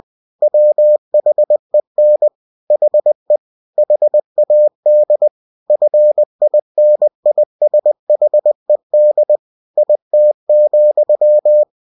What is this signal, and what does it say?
Morse code, heard as a keyed tone, about 20 wpm, 600 Hz. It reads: WHEN HE HAD FINISHED IT,
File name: cw-books / winnie-the-pooh / winnie-20wpm-page-0045